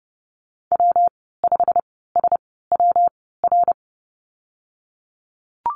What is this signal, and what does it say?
Morse code, heard as a keyed tone, about 30 wpm, 700 Hz. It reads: W 5 S W R
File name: Lesson 19g - Letter w - Call Signs 30wpm